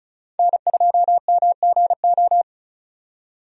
Morse code, heard as a keyed tone, about 36 wpm, 700 Hz. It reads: N2MGO
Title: USA Call Signs - 5 Characters - 35wpm